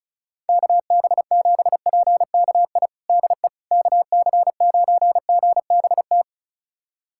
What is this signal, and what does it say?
Morse code, heard as a keyed tone, about 36 wpm, 700 Hz. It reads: KB7PKI DE KC9GBT
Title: USA Callsign DE Callsign - 6 Characters - 35wpm